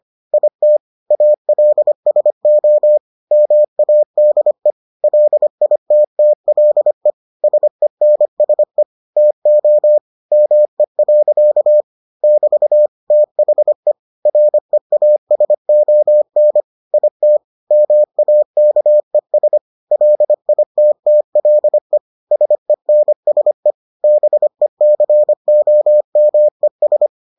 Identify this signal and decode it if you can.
Morse code, heard as a keyed tone, about 25 wpm, 600 Hz. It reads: IT ALSO MADE LITTLE SENSE TO ME. <BT> THE REASON IT MAKES LITTLE SENSE BECOMES